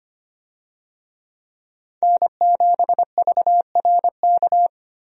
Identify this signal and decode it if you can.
Morse code, heard as a keyed tone, about 25 wpm, 700 Hz. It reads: N7VRK
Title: USA Call Signs - 5 Characters - Repeated 2x - 25wpm